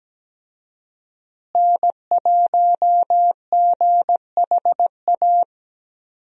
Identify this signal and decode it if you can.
Morse code, heard as a keyed tone, about 17 wpm, 700 Hz. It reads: N1GHA
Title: USA Call Signs - 5 Characters - Repeated 2x - 17wpm